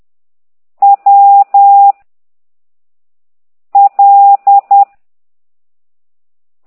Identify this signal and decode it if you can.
Morse code, heard as a keyed tone, about 10 wpm, 800 Hz. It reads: W L